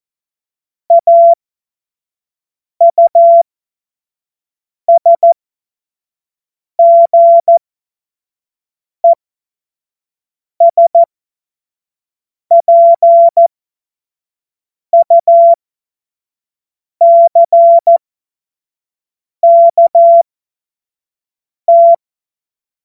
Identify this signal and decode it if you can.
Morse code, heard as a keyed tone, about 14 wpm, spaced as for 5 wpm, 680 Hz. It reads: AUSGESPUCKT
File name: Morsekurs / Witze / Witz 19. Oktober 2021, 25 F70